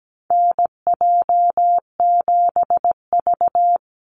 Morse code, heard as a keyed tone, about 17 wpm, 700 Hz. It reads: NJ7V
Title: NA SOTA Activator Callsigns 17wpm